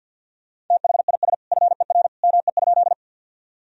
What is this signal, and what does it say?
Morse code, heard as a keyed tone, about 52 wpm, 700 Hz. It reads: THIS FER ME?
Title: Sentences from Top 200 Words and Abbreviations 50wpm